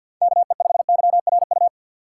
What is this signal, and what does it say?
Morse code, heard as a keyed tone, about 47 wpm, 700 Hz. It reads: KE5YRU
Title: USA Call Signs - 6 Characters - 4x Character Spacing - 50wpm